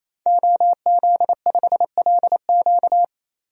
Morse code, heard as a keyed tone, about 28 wpm, 700 Hz. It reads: OZ5LQ
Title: DXCC-100 Call Signs - 5 Characters - 28wpm